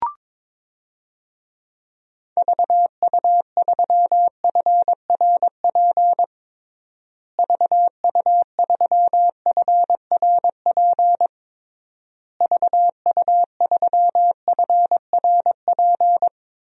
Morse code, heard as a keyed tone, about 22 wpm, 700 Hz. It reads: VU3FRP VU3FRP VU3FRP
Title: DXCC-100 Call Signs - 6 Characters - Repeated 3x - 22wpm